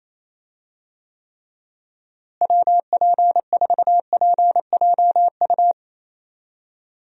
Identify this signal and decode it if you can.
Morse code, heard as a keyed tone, about 27 wpm, 700 Hz. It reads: WP4PJU
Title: USA Call Signs - 6 Characters - Repeated 2x - 28wpm